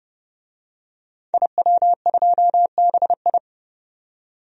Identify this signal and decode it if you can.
Morse code, heard as a keyed tone, about 30 wpm, 700 Hz. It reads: IW2BI
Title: DXCC-100 Call Signs - 5 Characters - 30wpm